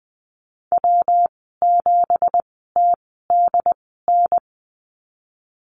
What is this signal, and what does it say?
Morse code, heard as a keyed tone, about 20 wpm, spaced as for 14 wpm, 700 Hz. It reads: W7TDN
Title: 245 Lesson 39g - Letter z - Call Signs 20wpm